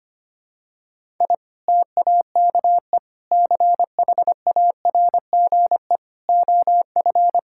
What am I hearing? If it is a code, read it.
Morse code, heard as a keyed tone, about 25 wpm, 700 Hz. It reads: I TAKE CHARGE OF
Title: Sentences from Top 1000 Words - Review Part 6 - 25wpm